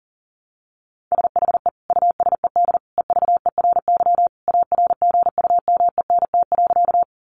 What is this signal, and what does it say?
Morse code, heard as a keyed tone, about 40 wpm, 700 Hz. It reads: SHE USED EVERY ARGUMENT.
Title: Sentences from Top 1400 Words 40wpm